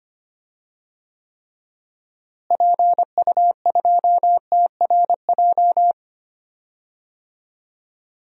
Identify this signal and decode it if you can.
Morse code, heard as a keyed tone, about 25 wpm, 700 Hz. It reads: PU2TRJ
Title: DXCC-100 Call Signs - 6 Characters - Repeated 3x - 25wpm